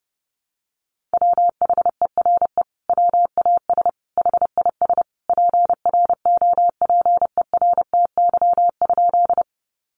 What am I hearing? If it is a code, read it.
Morse code, heard as a keyed tone, about 30 wpm, 700 Hz. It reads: WHERE WAS HIS PROPERTY?
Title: Sentences from Top 1000 Words 30wpm